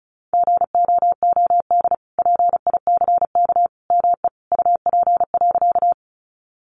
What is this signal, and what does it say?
Morse code, heard as a keyed tone, about 34 wpm, 700 Hz. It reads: GOOD PICK ME UP.
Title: Sentences from Top 1000 Words 35wpm